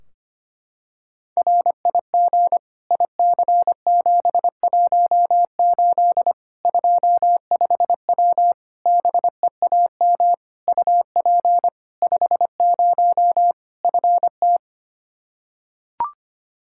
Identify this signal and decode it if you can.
Morse code, heard as a keyed tone, about 25 wpm, 700 Hz. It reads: RIG IC718 25W BEAM UP 50 FT
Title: K1USN SST QSO Practice 25wpm